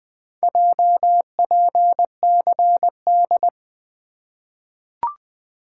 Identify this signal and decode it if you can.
Morse code, heard as a keyed tone, about 20 wpm, 700 Hz. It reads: JPCD
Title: Four Letter-Number Combinations - 20wpm